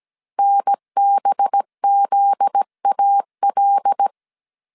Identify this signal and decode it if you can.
Morse code, heard as a keyed tone, about 17 wpm, 800 Hz. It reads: NBZAL
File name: CW / DET / 1 9 1500C S C 16wpm 2909